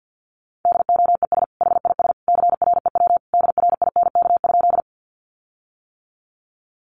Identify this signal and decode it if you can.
Morse code, heard as a keyed tone, about 48 wpm, 700 Hz. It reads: DOES HIS CREW DRINK?
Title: Sentences from Top 1500 Words 50wpm